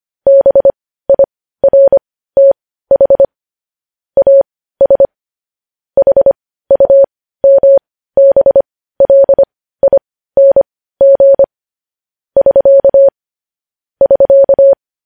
Morse code, heard as a keyed tone, about 25 wpm, spaced as for 15 wpm, 550 Hz. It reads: BIRTH AS HUMBLING <SK> <SK>
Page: news-headlines-in-morse-code-at-15-wpm